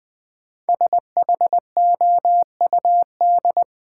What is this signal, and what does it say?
Morse code, heard as a keyed tone, about 21 wpm, 700 Hz. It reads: SHOUD
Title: Sentences from Top 1000 Words - Review Part 4 - New Word Drill - 20wpm